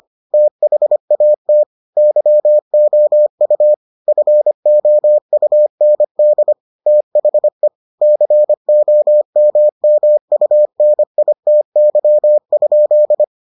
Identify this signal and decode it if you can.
Morse code, heard as a keyed tone, about 25 wpm, 600 Hz. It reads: THAT YOU FOUND THE COMMUNITY?